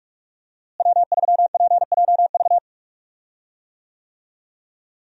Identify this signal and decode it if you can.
Morse code, heard as a keyed tone, about 46 wpm, 700 Hz. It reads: W3PJV
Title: USA Call Signs - 5 Characters - Repeated 3x - 45wpm